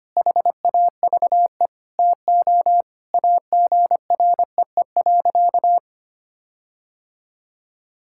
Morse code, heard as a keyed tone, about 25 wpm, 700 Hz. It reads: HAVE TO AGREE.
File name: practice / 25wpm / Sentences with 4-Word Phrases - 25wpm